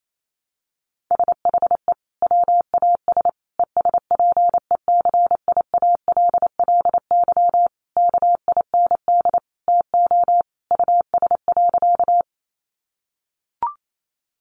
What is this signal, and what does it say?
Morse code, heard as a keyed tone, about 28 wpm, 700 Hz. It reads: SHE WAS ESPECIALLY KIND TO US.
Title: Sentences from Top 700 Words 28wpm